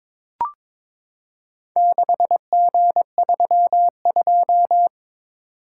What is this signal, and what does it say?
Morse code, heard as a keyed tone, about 22 wpm, 700 Hz. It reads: 6G32
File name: practice / 22wpm / Four Letter-Number Combinations - 22wpm